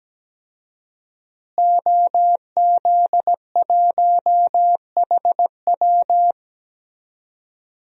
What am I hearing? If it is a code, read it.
Morse code, heard as a keyed tone, about 17 wpm, 700 Hz. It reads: OZ1HW
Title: DXCC-100 Call Signs - 5 Characters - Repeated 3x - 17wpm